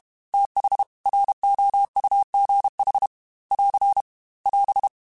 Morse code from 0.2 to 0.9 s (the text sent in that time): TH